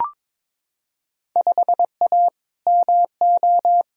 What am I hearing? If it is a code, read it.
Morse code, heard as a keyed tone, about 21 wpm, 700 Hz. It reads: 5A MO
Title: Field Day Exchanges Encore - 22wpm